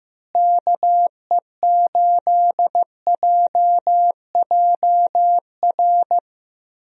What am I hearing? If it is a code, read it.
Morse code, heard as a keyed tone, about 15 wpm, 700 Hz. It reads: KE8JJR
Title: USA Call Signs - 6 Characters - Repeated 3x - 15wpm